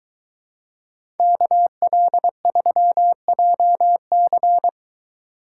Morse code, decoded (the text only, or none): KL3JC